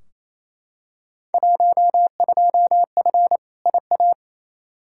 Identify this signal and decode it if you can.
Morse code, heard as a keyed tone, about 28 wpm, 700 Hz. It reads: 12F IA
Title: Field Day Exchanges Encore - 28wpm